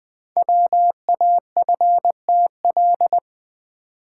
Morse code, heard as a keyed tone, about 20 wpm, 700 Hz. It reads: WAFTL
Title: Code Groups - 5 Letters - 20wpm